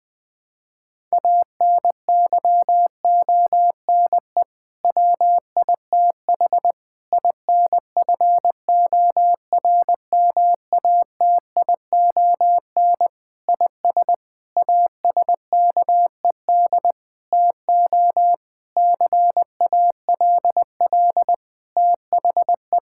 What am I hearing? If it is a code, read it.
Morse code, heard as a keyed tone, about 20 wpm, 700 Hz. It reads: ANYONE WITH INFORMATION IS ASKED TO CALL THE
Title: Sentences with 6-Word Phrases - Spoken Beginning - 20wpm